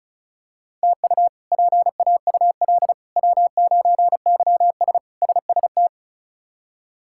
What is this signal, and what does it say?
Morse code, heard as a keyed tone, about 34 wpm, 700 Hz. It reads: TU PAUL W9YS SST
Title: K1USN SST QSO Practice - 35wpm